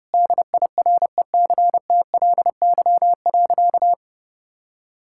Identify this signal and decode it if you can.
Morse code, heard as a keyed tone, about 32 wpm, 700 Hz. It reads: DIRECTLY.
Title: Sentences from Top 1000 Words - Review Part 5 - 30wpm